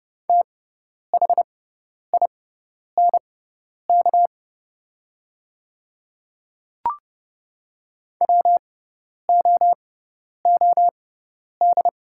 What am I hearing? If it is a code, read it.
Morse code, heard as a keyed tone, about 30 wpm, spaced as for 10 wpm, 700 Hz. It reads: THINK WOOD
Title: 6x Character Spacing - Top 300 Words - 30wpm